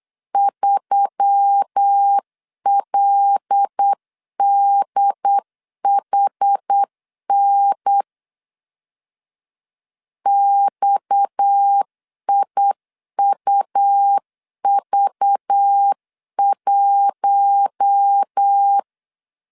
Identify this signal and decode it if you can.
Morse code, heard as a keyed tone, about 8 wpm, 800 Hz. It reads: 3LDHN XIUV1
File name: CW / DET / 1 6 1500C S C 8wpm 5219